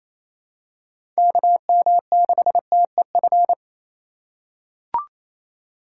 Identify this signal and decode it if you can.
Morse code, heard as a keyed tone, about 28 wpm, 700 Hz. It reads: KM6TEF
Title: USA Call Signs - 6 Characters - 2x Character Spacing - 28wpm